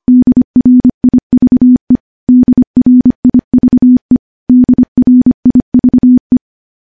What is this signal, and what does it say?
Morse code, heard as a keyed tone, about 25 wpm, 250 Hz. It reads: DRIVE DRIVE DRIVE
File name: morse-training-25WPM